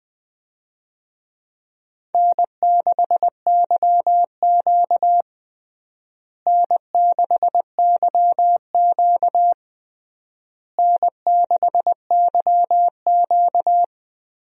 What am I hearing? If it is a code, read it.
Morse code, heard as a keyed tone, about 20 wpm, 700 Hz. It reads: N6YQ N6YQ N6YQ
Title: USA Call Signs - 4 Characters - Repeated 3x - 20wpm